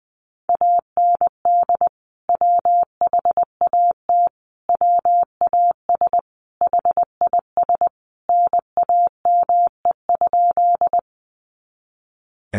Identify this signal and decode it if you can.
Morse code, heard as a keyed tone, about 20 wpm, 700 Hz. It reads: AND WHAT WAS HIS NAME?
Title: Sentences from Top 200 Words 20wpm